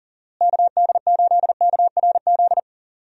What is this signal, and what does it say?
Morse code, heard as a keyed tone, about 40 wpm, 700 Hz. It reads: KD8KRZ
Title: USA Call Signs - 6 Characters - Repeated 2x - 40wpm